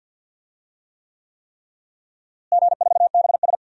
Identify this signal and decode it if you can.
Morse code, heard as a keyed tone, about 52 wpm, 700 Hz. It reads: G4BS
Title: DXCC-100 Call Signs - 4 Characters - Repeated 3x - 50wpm